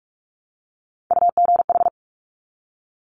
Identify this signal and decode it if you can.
Morse code, heard as a keyed tone, about 46 wpm, 700 Hz. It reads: UGH
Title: Top Three Letter Combinations in Words 45wpm